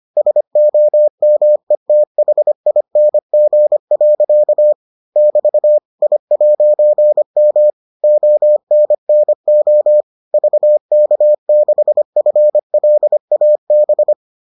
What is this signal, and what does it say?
Morse code, heard as a keyed tone, about 25 wpm, 600 Hz. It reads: SOMETHING. <BT> I'M ONNO VK6FLAB